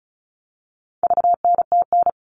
Morse code, heard as a keyed tone, about 34 wpm, 700 Hz. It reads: VNTN
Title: All Cut Numbers - 4-Digit 35wpm